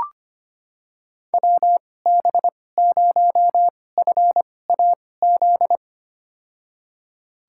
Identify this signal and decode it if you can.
Morse code, heard as a keyed tone, about 25 wpm, spaced as for 18 wpm, 700 Hz. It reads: WB0FAZ
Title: USA Call Signs - 6 Characters - 2x Character Spacing - 25wpm